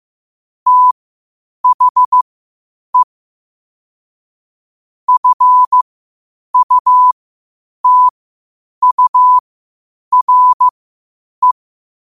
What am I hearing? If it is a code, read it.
Morse code, heard as a keyed tone, about 15 wpm, spaced as for 8 wpm, 1000 Hz. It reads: THE FUTURE
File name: qotdfiles / 8wpm